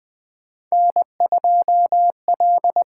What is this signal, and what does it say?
Morse code, heard as a keyed tone, about 20 wpm, 700 Hz. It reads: N2L
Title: USA Call Signs - 8x Character Spacing - 20wpm